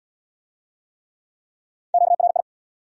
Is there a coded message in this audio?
Yes: Morse code from 1.9 to 2.4 s, heard as a keyed tone, about 72 wpm, 700 Hz. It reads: QRI